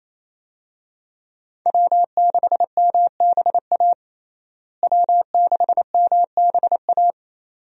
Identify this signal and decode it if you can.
Morse code, heard as a keyed tone, about 28 wpm, 700 Hz. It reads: W6MBA W6MBA
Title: USA Call Signs - 5 Characters - Repeated 2x - 28wpm